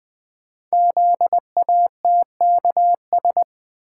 Morse code, heard as a keyed tone, about 20 wpm, 700 Hz. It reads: ZATKS
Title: Code Groups - 5 Letters - 20wpm